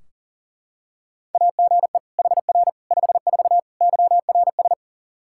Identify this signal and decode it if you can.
Morse code, heard as a keyed tone, about 40 wpm, 700 Hz. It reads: AGE HR 54 YRS
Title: Practice QSO - Exchange 3 - More Info - 40wpm